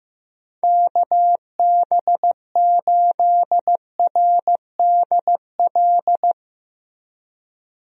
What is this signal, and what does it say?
Morse code, heard as a keyed tone, about 15 wpm, 700 Hz. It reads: KB8RDL
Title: USA Call Signs - 6 Characters - Repeated 2x - 15wpm